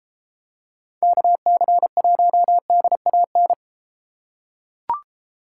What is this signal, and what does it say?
Morse code, heard as a keyed tone, about 32 wpm, 700 Hz. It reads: KC1DAN